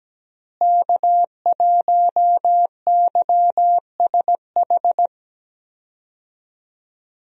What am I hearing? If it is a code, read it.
Morse code, heard as a keyed tone, about 17 wpm, 700 Hz. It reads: K1YSH
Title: USA Call Signs - 5 Characters - Repeated 2x - 17wpm